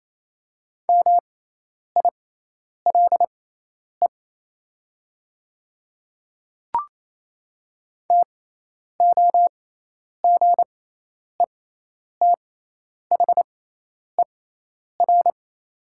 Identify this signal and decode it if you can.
Morse code, heard as a keyed tone, about 28 wpm, spaced as for 10 wpm, 700 Hz. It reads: MILE TOGETHER
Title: 6x Character Spacing - Top 300 Words - 28wpm